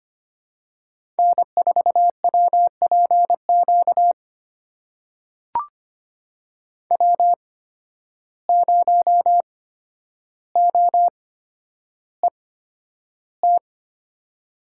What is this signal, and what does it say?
Morse code, heard as a keyed tone, about 25 wpm, spaced as for 11 wpm, 700 Hz. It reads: N4WPQ W0OET
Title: USA Call Signs - 5 Characters - 8x Character Spacing - 25wpm